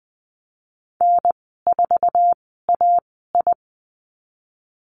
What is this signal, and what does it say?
Morse code, heard as a keyed tone, about 20 wpm, spaced as for 15 wpm, 700 Hz. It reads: N4AI
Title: 048 Lesson 9g - Number 4 - Call Signs 20wpm